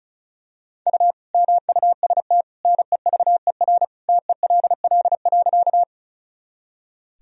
Morse code, heard as a keyed tone, about 36 wpm, 700 Hz. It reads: U MUST NEVER TELL.
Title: Sentences from Top 300 Words and Abbreviations 35wpm